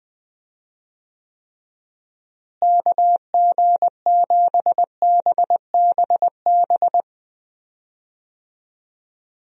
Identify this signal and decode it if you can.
Morse code, heard as a keyed tone, about 20 wpm, 700 Hz. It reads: KG7BBB
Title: USA Call Signs - 6 Characters - Repeated 2x - 20wpm